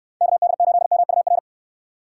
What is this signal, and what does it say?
Morse code, heard as a keyed tone, about 62 wpm, 700 Hz. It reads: KD8MUR